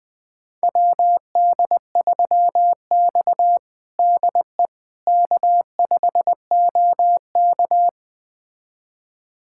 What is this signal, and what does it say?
Morse code, heard as a keyed tone, about 20 wpm, 700 Hz. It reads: WD3X DE K5OK
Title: USA Callsign DE Callsign - 4 Characters - 20wpm